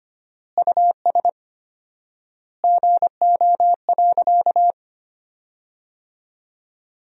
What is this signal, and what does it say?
Morse code, heard as a keyed tone, about 25 wpm, 700 Hz. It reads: US GO.
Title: 4x Word-Spacing - Sentences from Top 400 Words - 25wpm